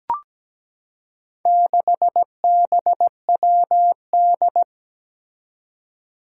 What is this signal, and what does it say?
Morse code, heard as a keyed tone, about 17 wpm, 700 Hz. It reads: 6BWD
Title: Four Challenging Letter-Number Combinations 17wpm